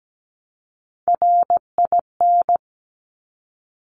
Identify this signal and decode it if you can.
Morse code, heard as a keyed tone, about 17 wpm, 700 Hz. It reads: RIN
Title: Top Three Letter Combinations in Words 17wpm